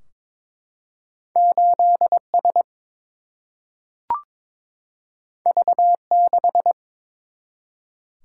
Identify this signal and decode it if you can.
Morse code, heard as a keyed tone, about 22 wpm, 700 Hz. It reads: 8 S V 6
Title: Single-Letter Single-Number 22wpm